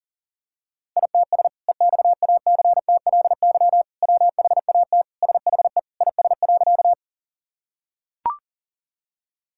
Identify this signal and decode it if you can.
Morse code, heard as a keyed tone, about 40 wpm, 700 Hz. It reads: ITS EXACTLY WHAT SHE IS.
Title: Sentences from Top 1000 Words - Review Part 6 - 40wpm